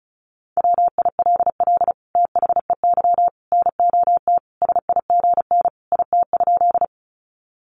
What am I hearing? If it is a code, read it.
Morse code, heard as a keyed tone, about 34 wpm, 700 Hz. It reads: WILL THEY NOT SIGN IT?
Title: Sentences from Top 1000 Words 35wpm